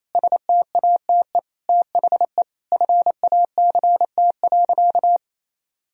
Morse code, heard as a keyed tone, about 28 wpm, 700 Hz. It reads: STATE THE FACT.
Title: Sentences from Top 1000 Words - Review Part 4 - 28wpm